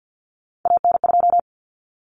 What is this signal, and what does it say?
Morse code, heard as a keyed tone, about 50 wpm, 700 Hz. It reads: AN2
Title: Common Cut Numbers - 3-Digit 50wpm